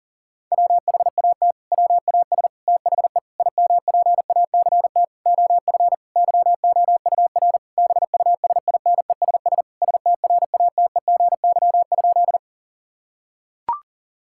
Morse code, heard as a keyed tone, about 40 wpm, 700 Hz. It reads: WHAT WAS THE IMPACT OF YOUR BUSINESS STRATEGY?
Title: Sentences from Top 1000 Words - Review Part 6 - 40wpm